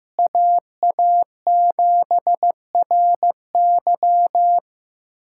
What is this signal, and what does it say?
Morse code, heard as a keyed tone, about 15 wpm, 700 Hz. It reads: AA7RY
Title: USA Callsign DE Callsign - 5 Characters - 15wpm